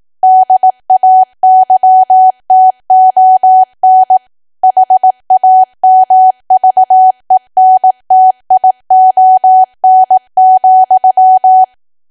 Morse code, heard as a keyed tone, about 18 wpm, 750 Hz. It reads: DAYTON HAMVENTION,
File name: Archive / 18 WPM / 18WPM